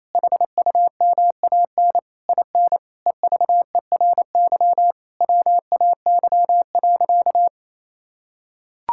Morse code, heard as a keyed tone, about 28 wpm, 700 Hz. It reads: HUMAN IN EVERY WAY.
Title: Sentences from Top 1000 Words - Review Part 6 - 28wpm